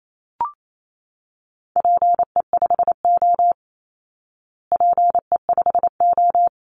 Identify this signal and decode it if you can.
Morse code, heard as a keyed tone, about 28 wpm, 700 Hz. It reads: PE5O PE5O
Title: Call Signs Repeated 3x - 28wpm